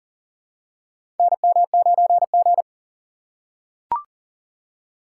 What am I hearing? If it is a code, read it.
Morse code, heard as a keyed tone, about 40 wpm, 700 Hz. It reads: NM9G
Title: USA Call Signs - 4 Characters - 3x Character Spacing - 40wpm